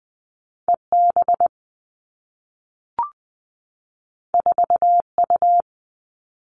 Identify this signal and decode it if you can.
Morse code, heard as a keyed tone, about 20 wpm, 700 Hz. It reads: E B 4 U
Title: Two Challenging Letter-Number Combinations 20wpm